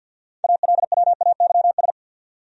Morse code, heard as a keyed tone, about 68 wpm, spaced as for 59 wpm, 700 Hz. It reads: ALWAYS
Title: Speed-Racing Top 300 Words 50wpm